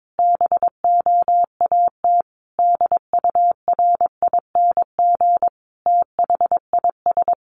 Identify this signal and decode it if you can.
Morse code, heard as a keyed tone, about 22 wpm, 700 Hz. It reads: BOAT DURING THIS
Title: Sentences from Top 1500 Words 22wpm